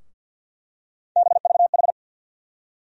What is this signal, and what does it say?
Morse code, heard as a keyed tone, about 52 wpm, 700 Hz. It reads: BVH
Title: USA Call Sign Postfixes - 3 Letters 50wpm